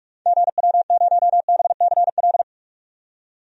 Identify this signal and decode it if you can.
Morse code, heard as a keyed tone, about 46 wpm, 700 Hz. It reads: GW0BCL
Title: DXCC-100 Call Signs - 6 Characters - 45wpm